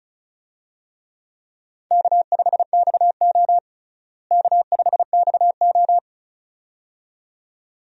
Morse code, heard as a keyed tone, about 36 wpm, 700 Hz. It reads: K5XO K5XO
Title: USA Call Signs - 4 Characters - Repeated 2x - 35wpm